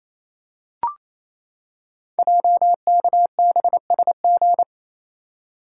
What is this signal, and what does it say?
Morse code, heard as a keyed tone, about 28 wpm, 700 Hz. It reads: JKBSG
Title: Code Groups - 5 Letters - 28wpm